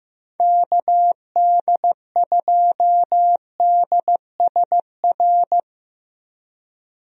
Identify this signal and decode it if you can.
Morse code, heard as a keyed tone, about 15 wpm, 700 Hz. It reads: KD2DSR
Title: USA Call Signs - 6 Characters - 15wpm